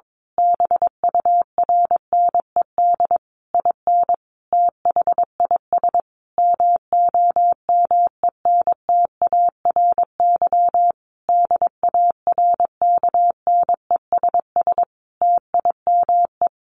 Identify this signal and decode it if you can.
Morse code, heard as a keyed tone, about 22 wpm, 700 Hz. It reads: BURNED IN THIS MOMENTARY DARKNESS TIME
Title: A Sound Of Thunder 22wpm